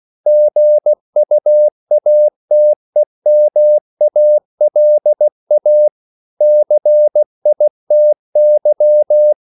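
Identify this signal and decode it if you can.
Morse code, heard as a keyed tone, about 16 wpm, 600 Hz. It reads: GUATEMALA CITY
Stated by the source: CW practice of national capitals at 16 words per minute, from ON6ZQ.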